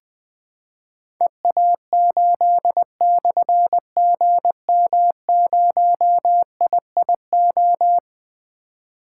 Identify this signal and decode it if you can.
Morse code, heard as a keyed tone, about 20 wpm, 700 Hz. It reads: EA8/GM0IIO
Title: Difficult Call Signs 20wpm